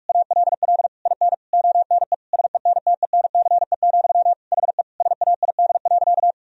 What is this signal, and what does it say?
Morse code, heard as a keyed tone, about 46 wpm, 700 Hz. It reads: ALL IN ONE SENTENCE, HE SAID.
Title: Sentences from Top 1000 Words - Review Part 1 - 45wpm